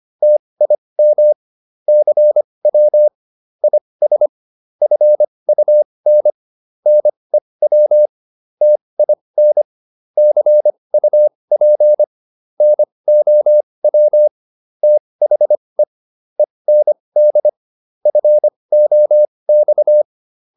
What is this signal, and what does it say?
Morse code, heard as a keyed tone, about 25 wpm, spaced as for 20 wpm, 600 Hz. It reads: TIM CW IS FUN NEW TIN CUP NOW THE END FOX